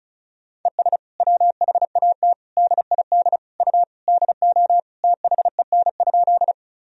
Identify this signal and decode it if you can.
Morse code, heard as a keyed tone, about 36 wpm, 700 Hz. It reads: ES WHAT DID U DO THEN?
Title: Sentences from Top 100 Words and Abbreviations 35wpm